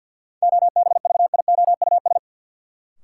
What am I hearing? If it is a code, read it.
Morse code, heard as a keyed tone, about 53 wpm, 700 Hz. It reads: OBVIOUS
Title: Sentences from Top 1600 Words - New Word Drill Alt - 50wpm